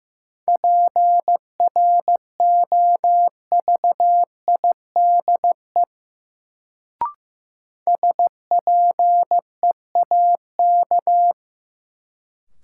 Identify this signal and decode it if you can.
Morse code, heard as a keyed tone, about 15 wpm, 700 Hz. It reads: PROVIDE SPEAK
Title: Sentences from Top 1100 Words - New Word Drill Alt - 15wpm